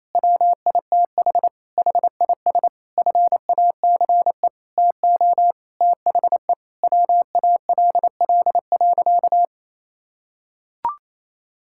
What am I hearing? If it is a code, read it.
Morse code, heard as a keyed tone, about 28 wpm, 700 Hz. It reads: WITH HIS FACE TO THE WALL.
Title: Sentences from Top 1000 Words - Review Part 6 - 28wpm